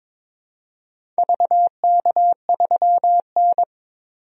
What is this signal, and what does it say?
Morse code, heard as a keyed tone, about 22 wpm, 700 Hz. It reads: VK3N
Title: DXCC-100 Call Signs - 4 Characters - 22wpm